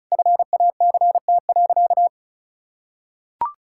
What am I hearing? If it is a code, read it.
Morse code, heard as a keyed tone, about 34 wpm, 700 Hz. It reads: FACT.
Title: Sentences from Top 1000 Words - Review Part 6 - 35wpm